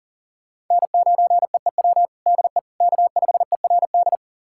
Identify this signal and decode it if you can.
Morse code, heard as a keyed tone, about 40 wpm, 700 Hz. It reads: N9EEW DE K5ERD
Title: USA Callsign DE Callsign - 5 Characters - 40wpm